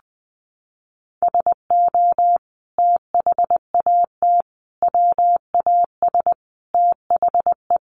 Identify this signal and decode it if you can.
Morse code, heard as a keyed tone, about 20 wpm, 700 Hz. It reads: SO THAT WAS THE
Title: Sentences from Top 100 Words 20wpm